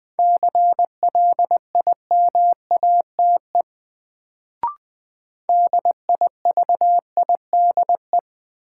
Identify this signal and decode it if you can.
Morse code, heard as a keyed tone, about 20 wpm, 700 Hz. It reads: CLIMATE DIVIDE
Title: Sentences from Top 1600 Words - New Word Drill Alt - 20wpm